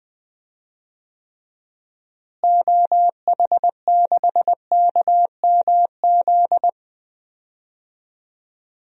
Morse code, heard as a keyed tone, about 20 wpm, 700 Hz. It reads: OH6KMZ
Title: DXCC-100 Call Signs - 6 Characters - Repeated 3x - 20wpm